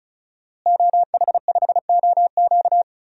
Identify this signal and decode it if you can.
Morse code, heard as a keyed tone, about 36 wpm, 700 Hz. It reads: OH5OQ
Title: DXCC-100 Call Signs - 5 Characters - Repeated 3x - 35wpm